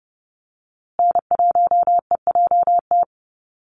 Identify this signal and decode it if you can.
Morse code, heard as a keyed tone, about 30 wpm, 700 Hz. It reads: N1EJT
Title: USA Call Signs Repeated 2x 30wpm